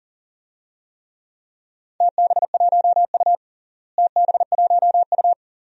Morse code, heard as a keyed tone, about 40 wpm, 700 Hz. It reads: TB1U TB1U
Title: DXCC-100 Call Signs - 4 Characters - Repeated 2x - 40wpm